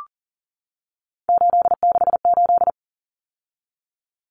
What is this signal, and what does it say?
Morse code, heard as a keyed tone, about 40 wpm, 700 Hz. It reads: 868
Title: Numbers - 3-Digit 40wpm